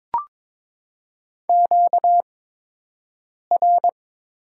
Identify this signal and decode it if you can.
Morse code, heard as a keyed tone, about 22 wpm, spaced as for 6 wpm, 690 Hz. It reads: QR
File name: Top 90 QSO Elements - 8x Character Spacing - 22wpm